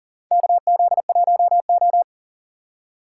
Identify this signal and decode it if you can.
Morse code, heard as a keyed tone, about 38 wpm, 700 Hz. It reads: KZ1O
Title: USA Call Signs - 4 Characters - 2x Character Spacing - 40wpm